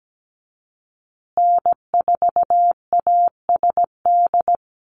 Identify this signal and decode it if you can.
Morse code, heard as a keyed tone, about 17 wpm, 700 Hz. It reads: N4ASD
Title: USA Call Signs Repeated 2x 17wpm